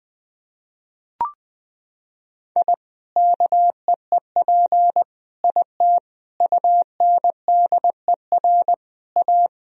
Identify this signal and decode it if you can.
Morse code, heard as a keyed tone, about 21 wpm, 700 Hz. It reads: I KEEP IT UNDER A
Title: Sentences from Top 1000 Words - Review Part 2 - 20wpm